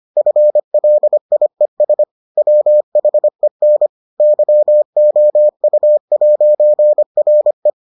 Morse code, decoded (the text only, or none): FLIES WHEN YOU'RE